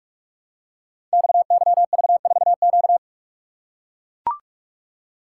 Morse code, heard as a keyed tone, about 44 wpm, 700 Hz. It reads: XYV4Q
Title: Code Groups - 5 Letter:Numbers - 45wpm